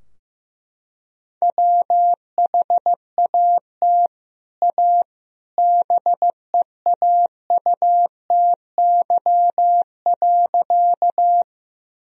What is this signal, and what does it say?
Morse code, heard as a keyed tone, about 15 wpm, 700 Hz. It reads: WHAT A BEAUTY.
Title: Sentences from Top 1000 Words - Review Part 2 - 15wpm